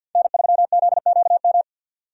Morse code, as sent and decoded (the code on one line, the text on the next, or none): -. ...-- --.. --.- --
N3ZQM